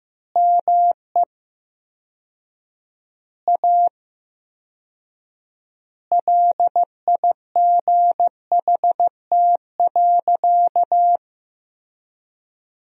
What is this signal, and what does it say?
Morse code, heard as a keyed tone, about 15 wpm, 700 Hz. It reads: ME A LIGHT.
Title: 4x Word-Spacing - Sentences from Top 200 Words - 15wpm